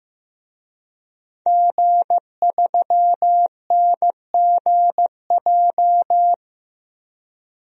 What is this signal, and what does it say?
Morse code, heard as a keyed tone, about 15 wpm, 700 Hz. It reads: G3NGJ
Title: DXCC-100 Call Signs - 5 Characters - 15wpm